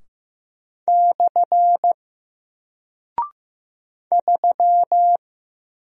Morse code, heard as a keyed tone, about 15 wpm, 700 Hz. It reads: / 3
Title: Single Character (Letter:Number:Punctuation:Prosign) Alt 15wpm